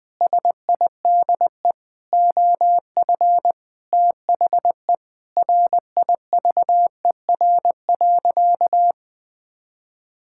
Morse code, heard as a keyed tone, about 20 wpm, 700 Hz. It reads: SIDE OF THE RIVER.
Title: Sentences with 6-Word Phrases - 20wpm